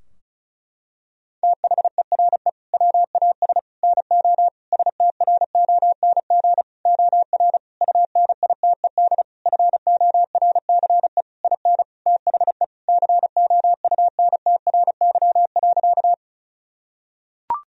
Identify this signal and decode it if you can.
Morse code, heard as a keyed tone, about 34 wpm, 700 Hz. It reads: THERE WAS NO STRONG OR UNITED FORCE IN THE COUNTRY.
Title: Sentences from Top 1000 Words - Review Part 4 - 35wpm